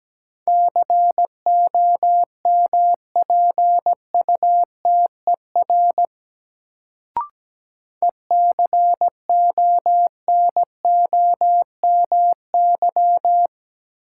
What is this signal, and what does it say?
Morse code, heard as a keyed tone, about 17 wpm, 700 Hz. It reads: COMPUTER ECONOMY
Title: Sentences from Top 1000 Words - Review Part 3 - New Word Drill - 17wpm